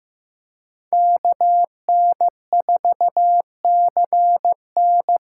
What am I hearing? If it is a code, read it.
Morse code, heard as a keyed tone, about 15 wpm, 700 Hz. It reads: KN4CN
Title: USA Call Signs - 8x Character Spacing - 15wpm